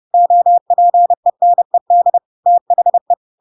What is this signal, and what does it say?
Morse code, heard as a keyed tone, about 30 wpm, 700 Hz. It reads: OPENED THE
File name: Pinocchio / Pinocchio ch6 30 30 ik1hge